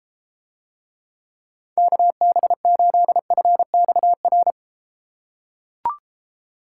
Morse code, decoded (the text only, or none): KB8FXR